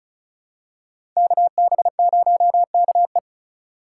Morse code, heard as a keyed tone, about 34 wpm, 700 Hz. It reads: KB0KE
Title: USA Call Signs - 5 Characters - 3x Character Spacing - 35wpm